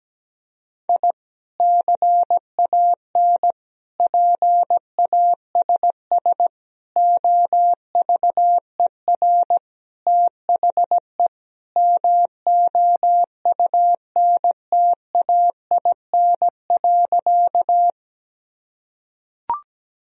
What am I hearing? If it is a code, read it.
Morse code, heard as a keyed tone, about 17 wpm, 700 Hz. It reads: I CAN PASS OVER THE MOUNTAIN.
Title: Sentences from Top 1000 Words - Review Part 1 - 17wpm